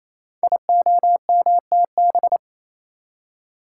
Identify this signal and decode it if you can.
Morse code, heard as a keyed tone, about 28 wpm, 700 Hz. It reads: IOMTB
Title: Code Groups - 5 Letters - 28wpm